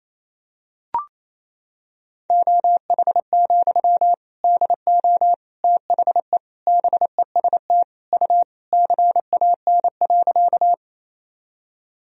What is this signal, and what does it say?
Morse code, heard as a keyed tone, about 28 wpm, 700 Hz. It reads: OH, DO THE BEST U CAN.
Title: Sentences from Top 500 Words and Abbreviations 28wpm